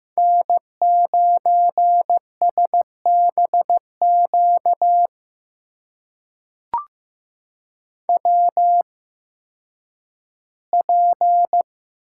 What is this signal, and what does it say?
Morse code, heard as a keyed tone, about 15 wpm, spaced as for 10 wpm, 700 Hz. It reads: N9SBQ WP
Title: USA Call Signs - 5 Characters - 8x Character Spacing - 15wpm